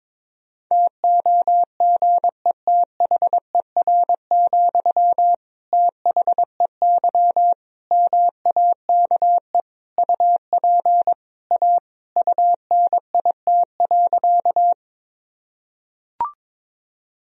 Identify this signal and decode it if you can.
Morse code, heard as a keyed tone, about 22 wpm, 700 Hz. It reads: TOGETHER, THEY MAKE UP A UNIT.
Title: Sentences from Top 1000 Words - Review Part 2 - 22wpm